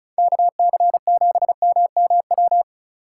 Morse code, heard as a keyed tone, about 34 wpm, 700 Hz. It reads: KC7MMW
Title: USA Call Signs - 6 Characters - 4x Character Spacing - 35wpm